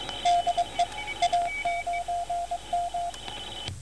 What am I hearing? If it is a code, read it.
Morse code, heard as a keyed tone, about 22 wpm, 700 Hz, with radio noise. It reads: DE A9M